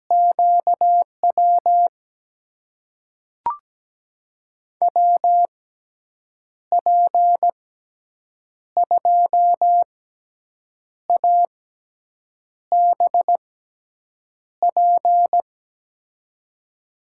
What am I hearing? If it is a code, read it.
Morse code, heard as a keyed tone, about 17 wpm, spaced as for 7 wpm, 700 Hz. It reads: QW WP2ABP
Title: USA Call Signs - 6 Characters - 6x Character Spacing - 17wpm